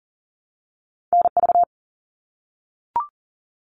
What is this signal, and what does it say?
Morse code, heard as a keyed tone, about 42 wpm, 700 Hz. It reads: NV
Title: US States 40wpm